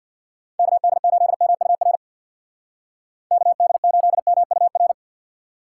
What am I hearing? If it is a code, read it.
Morse code, heard as a keyed tone, about 60 wpm, 700 Hz. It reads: KD8MUR KD8MUR
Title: Speed Racing - USA Call Signs 50wpm